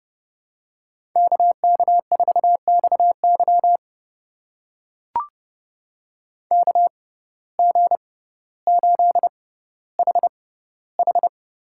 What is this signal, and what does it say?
Morse code, heard as a keyed tone, about 30 wpm, spaced as for 15 wpm, 700 Hz. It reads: KK4XY KG8HH
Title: USA Call Signs - 5 Characters - 6x Character Spacing - 30wpm